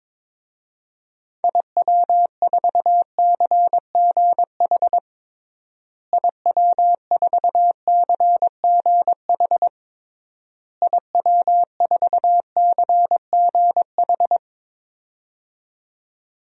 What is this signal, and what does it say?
Morse code, heard as a keyed tone, about 22 wpm, 700 Hz. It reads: IW4CGH IW4CGH IW4CGH
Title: DXCC-100 Call Signs - 6 Characters - Repeated 3x - 22wpm